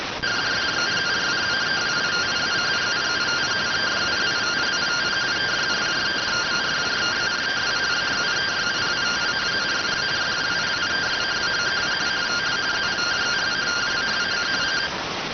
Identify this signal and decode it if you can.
Two-tone teleprinter signal, about 170 Hz shift.